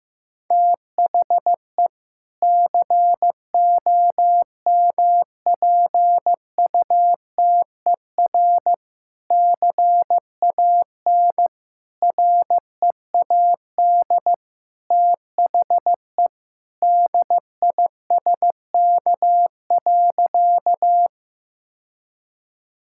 Morse code, heard as a keyed tone, about 16 wpm, 700 Hz. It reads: THE COMPUTER CAN READ THE DISK.
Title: Sentences from Top 1000 Words - Review Part 6 - 15wpm